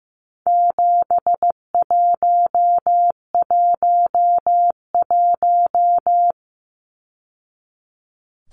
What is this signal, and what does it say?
Morse code, heard as a keyed tone, about 15 wpm, 700 Hz. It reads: 7111
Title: Numbers - 4-Digit 15wpm